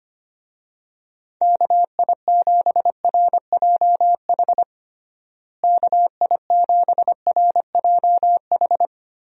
Morse code, heard as a keyed tone, about 25 wpm, 700 Hz. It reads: KI7RJH KI7RJH
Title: USA Call Signs - 6 Characters - Repeated 3x - 25wpm